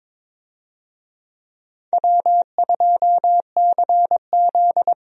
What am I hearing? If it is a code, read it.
Morse code, heard as a keyed tone, about 22 wpm, 700 Hz. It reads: W2CZ
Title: USA Call Signs - 4 Characters - Repeated 2x - 22wpm